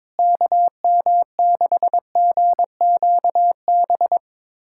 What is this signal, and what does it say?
Morse code, heard as a keyed tone, about 22 wpm, 700 Hz. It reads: KM6GQB
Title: USA Callsign DE Callsign 22wpm